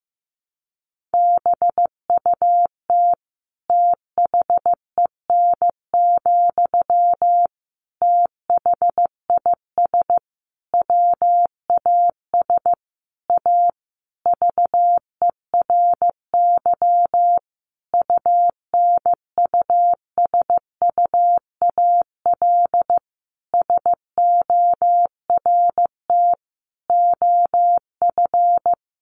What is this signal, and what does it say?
Morse code, heard as a keyed tone, about 15 wpm, 700 Hz. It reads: BUT THEN, THIS WAS A VERY UNUSUAL SORT OF
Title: Sentences from Top 1400 Words 15wpm